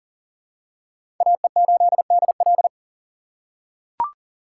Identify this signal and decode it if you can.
Morse code, heard as a keyed tone, about 40 wpm, 700 Hz. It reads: AE8DL
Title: USA Call Signs - 4x Character Spacing - 40wpm